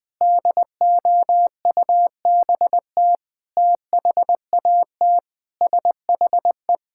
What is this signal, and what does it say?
Morse code, heard as a keyed tone, about 20 wpm, 700 Hz. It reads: DOUBT THAT SHE
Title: Sentences with 5-Word Phrases - Spoken Beginning - 20wpm